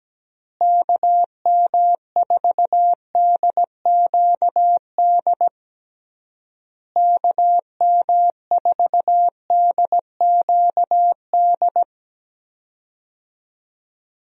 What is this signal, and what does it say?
Morse code, heard as a keyed tone, about 17 wpm, 700 Hz. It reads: KM4DQD KM4DQD
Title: USA Call Signs - 6 Characters - Repeated 3x - 17wpm